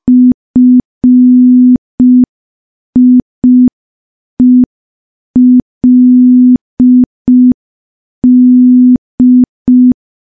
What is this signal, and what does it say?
Morse code, heard as a keyed tone, about 5 wpm, 250 Hz. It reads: FIELD
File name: morse / morse-training-05WPM